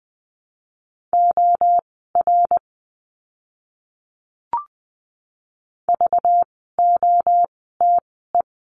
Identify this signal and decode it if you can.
Morse code, heard as a keyed tone, about 20 wpm, spaced as for 15 wpm, 700 Hz. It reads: OR VOTE
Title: 192 Lesson 31f - Letter v - Words 20wpm